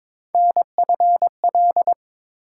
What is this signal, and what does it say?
Morse code, heard as a keyed tone, about 22 wpm, 700 Hz. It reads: NFL
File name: ARRL Sections v2 - 22wpm